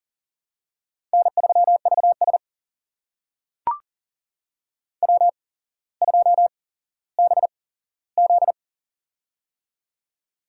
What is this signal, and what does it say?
Morse code, heard as a keyed tone, about 40 wpm, spaced as for 17 wpm, 700 Hz. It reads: N3VS W2BZ
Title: USA Call Signs - 4 Characters - 8x Character Spacing - 40wpm